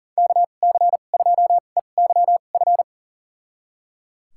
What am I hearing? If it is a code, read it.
Morse code, heard as a keyed tone, about 38 wpm, spaced as for 28 wpm, 700 Hz. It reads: KC2EYF
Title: USA Call Signs - 6 Characters - 2x Character Spacing - 40wpm